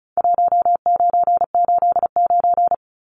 Morse code, heard as a keyed tone, about 34 wpm, 700 Hz. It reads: 1989
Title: Numbers - 4-Digit 35wpm